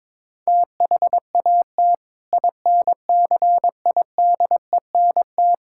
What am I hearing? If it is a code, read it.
Morse code, heard as a keyed tone, about 22 wpm, 700 Hz. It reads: THAT INCIDENT
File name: practice / 22wpm / Sentences with 8-Word Phrases - 22wpm